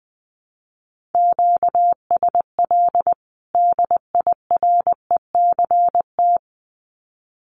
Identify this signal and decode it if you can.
Morse code, heard as a keyed tone, about 20 wpm, 700 Hz. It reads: QSL DIRECT
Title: Common QSO Phrases 20wpm